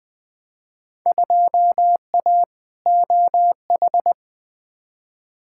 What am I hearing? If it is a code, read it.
Morse code, heard as a keyed tone, about 20 wpm, 700 Hz. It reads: 2A OH
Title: Field Day Exchanges - 20wpm